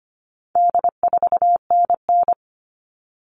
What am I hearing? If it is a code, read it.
Morse code, heard as a keyed tone, about 25 wpm, 700 Hz. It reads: D4NN
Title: All Cut Numbers - 4-Digit 25wpm